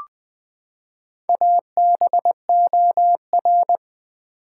Morse code, heard as a keyed tone, about 21 wpm, 700 Hz. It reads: ABOR
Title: Build A Word - 4 Letters At A Time - 8-Letter Words - 20wpm